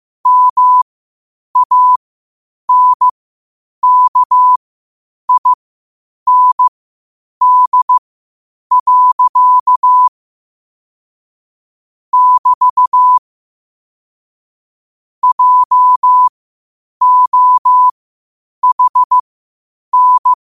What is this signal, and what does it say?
Morse code, heard as a keyed tone, about 15 wpm, spaced as for 8 wpm, 1000 Hz. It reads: MANKIND. <BT> JOHN
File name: qotdfiles / 8wpm